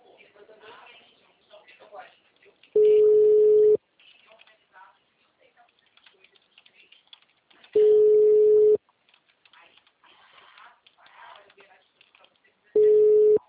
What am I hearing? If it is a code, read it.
Ringback tone (Europe, 425 Hz), with speech.